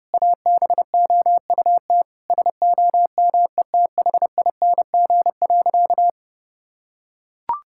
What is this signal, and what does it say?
Morse code, heard as a keyed tone, about 30 wpm, 700 Hz. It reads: ABOUT SOMETHING.
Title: Sentences from Top 1000 Words - Review Part 5 - 30wpm